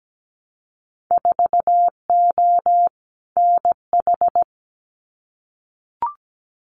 Morse code, heard as a keyed tone, about 17 wpm, 700 Hz. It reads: 4O NH
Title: Winter Field Day Exchanges Encore 17wpm